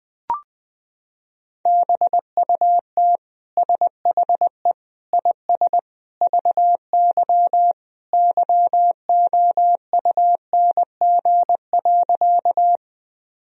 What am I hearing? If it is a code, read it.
Morse code, heard as a keyed tone, about 21 wpm, 700 Hz. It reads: BUT SHE IS VY YOUNG.
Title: Sentences from Top 400 Words and Abbreviations 20wpm